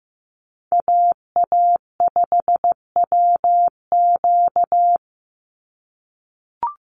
Morse code, heard as a keyed tone, about 15 wpm, 700 Hz. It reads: AA5WQ
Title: USA Call Signs 15wpm